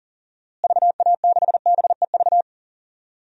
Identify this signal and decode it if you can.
Morse code, heard as a keyed tone, about 40 wpm, 700 Hz. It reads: VA6BEV
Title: DXCC-100 Call Signs - 6 Characters - Repeated 3x - 40wpm